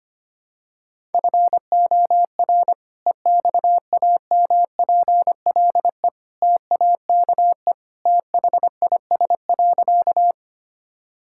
Morse code, heard as a keyed tone, about 25 wpm, 690 Hz. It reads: FOR EXAMPLE TAKE THIS.
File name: practice / 25wpm / Sentences with 2-Word Phrases - 25wpm